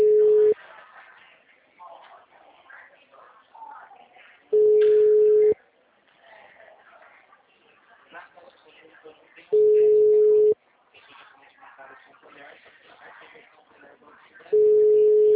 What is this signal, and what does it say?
Ringback tone (Europe, 425 Hz), with speech.